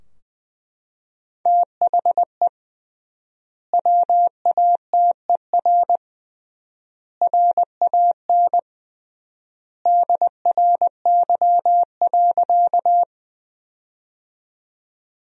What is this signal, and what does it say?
Morse code, heard as a keyed tone, about 21 wpm, 700 Hz. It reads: THE WATER RAN DRY.
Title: 3x Word-Spacing - Sentences from Top 500 Words - 20wpm